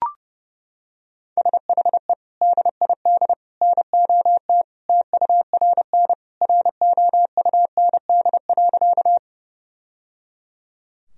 Morse code, heard as a keyed tone, about 30 wpm, 700 Hz. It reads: SHE DID NOT TURN ROUND.
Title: Sentences from Top 1000 Words - Review Part 1 - 30wpm